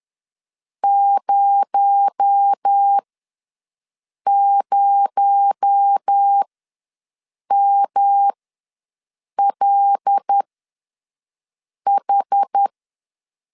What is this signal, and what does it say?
Morse code, heard as a keyed tone, about 10 wpm, spaced as for 5 wpm, 800 Hz. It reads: 00MLH
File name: LEZIONI / 6 PKX0 / 6 PKX0 ws 10wpm 4220